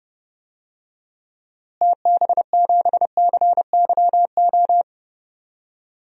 Morse code, heard as a keyed tone, about 30 wpm, 700 Hz. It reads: TB7CYO
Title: DXCC-100 Call Signs - 6 Characters - Repeated 3x - 30wpm